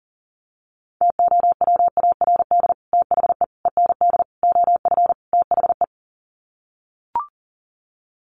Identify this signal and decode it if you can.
Morse code, heard as a keyed tone, about 40 wpm, 700 Hz. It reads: TOWARD THE END OF THE
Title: Sets of 5 Words 40wpm